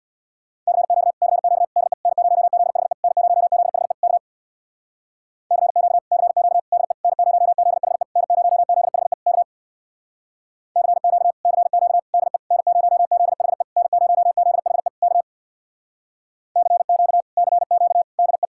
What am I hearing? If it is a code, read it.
Morse code, heard as a keyed tone, about 64 wpm, 700 Hz. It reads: CQ CQ DE N0ZHE N0ZHE K CQ CQ DE N0ZHE N0ZHE K CQ CQ DE N0ZHE N0ZHE K CQ CQ DE